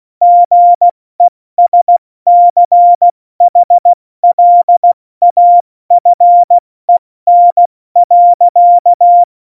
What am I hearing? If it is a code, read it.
Morse code, heard as a keyed tone, about 16 wpm, spaced as for 14 wpm, 700 Hz. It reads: GESCHLAFEN.
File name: Morsekurs / Witze / Witz 01. September 2020, 70 F80